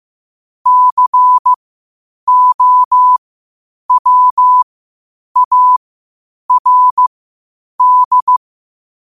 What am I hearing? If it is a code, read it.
Morse code, heard as a keyed tone, about 15 wpm, spaced as for 8 wpm, 1000 Hz. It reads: COWARD
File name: qotdfiles / 8wpm